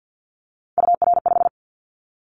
Morse code, heard as a keyed tone, about 50 wpm, 700 Hz. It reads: UR5
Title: Callsign Prefixes for Top 25 DXCC - 50wpm